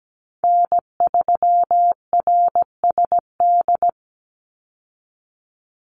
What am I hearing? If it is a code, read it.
Morse code, heard as a keyed tone, about 17 wpm, 700 Hz. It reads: N3RSD
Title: Call Signs Repeated 3x - 17wpm